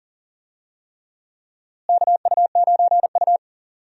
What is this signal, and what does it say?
Morse code, heard as a keyed tone, about 37 wpm, 700 Hz. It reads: KU9U
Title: USA Call Signs - 4 Characters - Repeated 3x - 40wpm